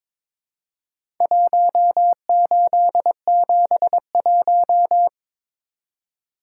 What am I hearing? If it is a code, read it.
Morse code, heard as a keyed tone, about 22 wpm, 700 Hz. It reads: 1871
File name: practice / 22wpm / Four Letter-Number Combinations - 22wpm